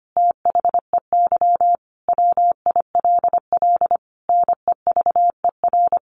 Morse code, heard as a keyed tone, about 25 wpm, 700 Hz. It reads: THEY WILL NEVER
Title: Sentences from Top 1400 Words 25wpm